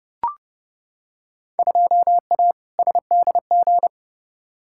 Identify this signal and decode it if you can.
Morse code, heard as a keyed tone, about 30 wpm, 700 Hz. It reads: 2A SDG
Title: Field Day Exchanges Encore - 30wpm